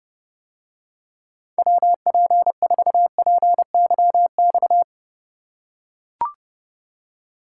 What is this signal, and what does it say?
Morse code, heard as a keyed tone, about 28 wpm, 700 Hz. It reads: WP4PYX